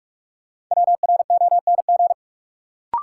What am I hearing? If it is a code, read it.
Morse code, heard as a keyed tone, about 47 wpm, 700 Hz. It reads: WRONG
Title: Sentences from Top 900 Words - New Word Drill - 45wpm